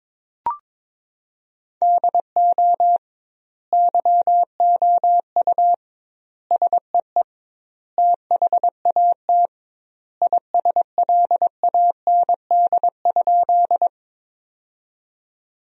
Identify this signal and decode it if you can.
Morse code, heard as a keyed tone, about 22 wpm, 700 Hz. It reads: DO YOU SEE THAT ISLAND?
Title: 2x Word-Spacing - Sentences from Top 500 Words - 22wpm